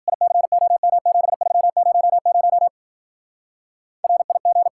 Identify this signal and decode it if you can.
Morse code, heard as a keyed tone, about 50 wpm, 700 Hz. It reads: ICOM7300 RIG
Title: Speed-Racing Common QSO Phrases - 40wpm